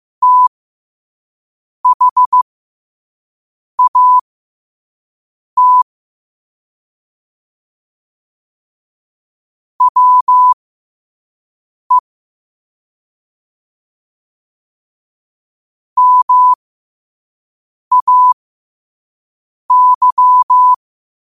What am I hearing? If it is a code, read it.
Morse code, heard as a keyed tone, about 15 wpm, spaced as for 5 wpm, 1000 Hz. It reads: THAT WE MAY
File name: qotdfiles / 5wpm